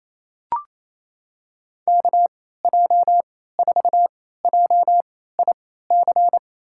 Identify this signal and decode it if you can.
Morse code, heard as a keyed tone, about 28 wpm, spaced as for 16 wpm, 700 Hz. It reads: KJ4JIC
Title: USA Call Signs - 6 Characters - 3x Character Spacing - 28wpm